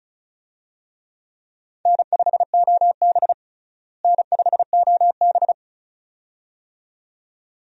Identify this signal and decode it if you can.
Morse code, heard as a keyed tone, about 36 wpm, 700 Hz. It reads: N5OB N5OB
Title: USA Call Signs - 4 Characters - Repeated 2x - 35wpm